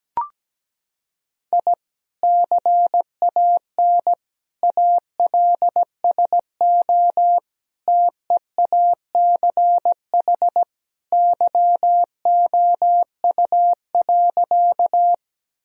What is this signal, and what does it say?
Morse code, heard as a keyed tone, about 17 wpm, 700 Hz. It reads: I CAN ALSO TEACH YOU.
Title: Sentences from Top 1000 Words - Review Part 2 - 17wpm